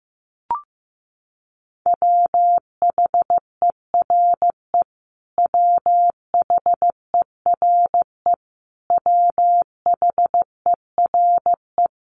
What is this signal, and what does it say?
Morse code, heard as a keyed tone, about 15 wpm, 700 Hz. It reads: WHERE WHERE WHERE
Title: Top 500 Words Repeated 15wpm